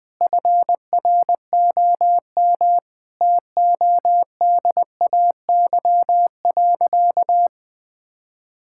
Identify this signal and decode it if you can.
Morse code, heard as a keyed tone, about 21 wpm, 700 Hz. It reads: FROM TODAY.
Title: Sentences from Top 1000 Words - Review Part 5 - 20wpm